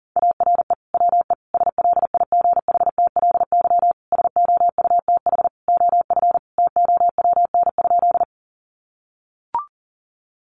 Morse code, heard as a keyed tone, about 40 wpm, 700 Hz. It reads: ARE WE SLIGHTLY SOUTH OF TOWN?
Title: Sentences from Top 1300 Words 40wpm